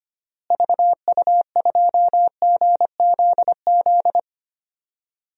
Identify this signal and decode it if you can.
Morse code, heard as a keyed tone, about 25 wpm, 700 Hz. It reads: VU2GZZ
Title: DXCC-100 Call Signs - 6 Characters - Repeated 3x - 25wpm